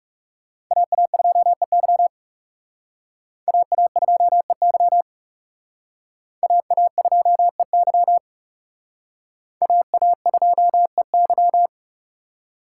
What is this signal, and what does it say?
Morse code, heard as a keyed tone, about 38 wpm, 700 Hz. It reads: AA2EY AA2EY AA2EY AA2EY